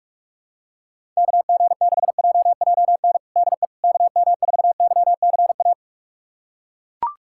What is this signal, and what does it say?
Morse code, heard as a keyed tone, about 41 wpm, 700 Hz. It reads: KG6JJN DE KM4YCA
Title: USA Callsign DE Callsign - 6 Characters - 45wpm